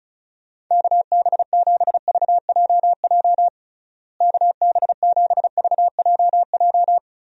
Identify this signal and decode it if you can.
Morse code, heard as a keyed tone, about 36 wpm, 700 Hz. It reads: KB7VJJ KB7VJJ
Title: USA Call Signs - 6 Characters - Repeated 3x - 35wpm